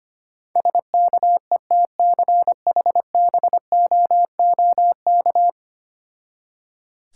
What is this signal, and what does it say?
Morse code, heard as a keyed tone, about 25 wpm, 700 Hz. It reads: SKETCHBOOK
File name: practice / 25wpm / Difficult Words with Hints - 25wpm